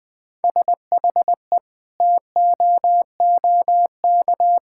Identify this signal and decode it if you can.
Morse code, heard as a keyed tone, about 20 wpm, 700 Hz. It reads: SHE TOOK
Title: Sentences from Top 1000 Words - Review Part 6 - 20wpm